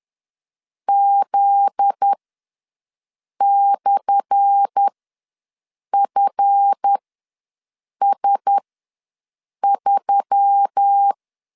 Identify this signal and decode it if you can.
Morse code, heard as a keyed tone, about 10 wpm, spaced as for 6 wpm, 800 Hz. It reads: Z/FS3
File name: LEZIONI / 10 SIGNS / 10C DOT ws 10wpm 4213